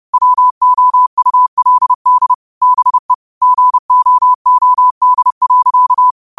Morse code, heard as a keyed tone, about 30 wpm, 1000 Hz. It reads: WOULD BE GOOD.